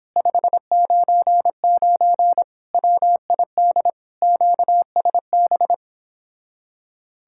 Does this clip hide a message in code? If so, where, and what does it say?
Yes: Morse code from 0.2 to 5.7 s, heard as a keyed tone, about 29 wpm, 700 Hz. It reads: 599 WID QSB